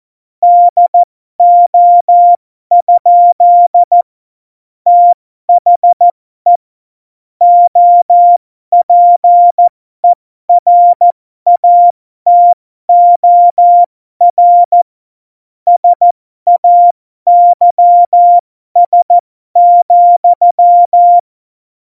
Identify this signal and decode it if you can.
Morse code, heard as a keyed tone, about 14 wpm, spaced as for 12 wpm, 700 Hz. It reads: DO? THE OPERATOR SAYS,
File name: Morsekurs / Witze